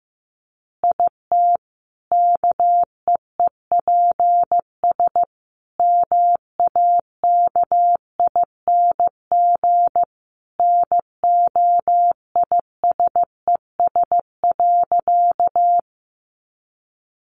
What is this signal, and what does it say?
Morse code, heard as a keyed tone, about 15 wpm, 700 Hz. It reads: IT KEEPS MAKING NOISES.
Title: Additional Easy Phrases 15wpm